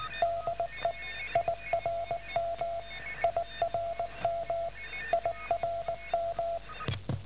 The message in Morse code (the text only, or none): DE IRM IRM IRM